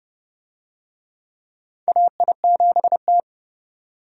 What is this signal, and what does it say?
Morse code, heard as a keyed tone, about 31 wpm, 700 Hz. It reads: AI7T
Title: USA Call Signs - 4 Characters - 30wpm